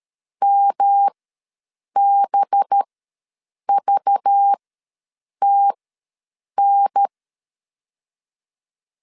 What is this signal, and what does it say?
Morse code, heard as a keyed tone, about 13 wpm, spaced as for 7 wpm, 800 Hz. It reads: MBVTN
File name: LEZIONI / 5 LBVC / 5 LBV ws 12wpm 3015